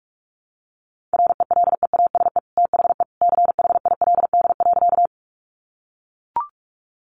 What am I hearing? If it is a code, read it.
Morse code, heard as a keyed tone, about 44 wpm, 700 Hz. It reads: RELEASE THE CHILD.
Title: Sentences from Top 1100 Words 45wpm